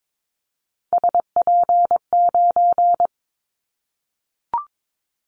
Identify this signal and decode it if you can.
Morse code, heard as a keyed tone, about 22 wpm, 700 Hz. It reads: SP9
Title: Callsign Prefixes for Top 25 DXCC - 22wpm